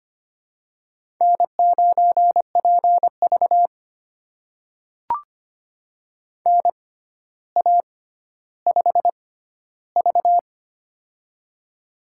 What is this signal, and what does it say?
Morse code, heard as a keyed tone, about 25 wpm, spaced as for 13 wpm, 700 Hz. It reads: N9PV NA5V
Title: USA Call Signs - 4 Characters - 6x Character Spacing - 25wpm